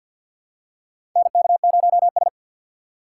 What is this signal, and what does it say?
Morse code, heard as a keyed tone, about 49 wpm, 700 Hz. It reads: NK0S
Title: USA Call Signs - 4x Character Spacing - 50wpm